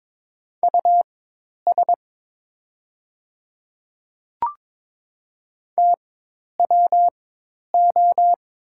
Morse code, heard as a keyed tone, about 22 wpm, spaced as for 10 wpm, 700 Hz. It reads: US TWO
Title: 4x Character Spacing - Top 300 Words - 22wpm